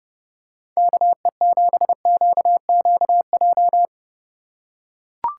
Morse code, heard as a keyed tone, about 31 wpm, 700 Hz. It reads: KE7QQJ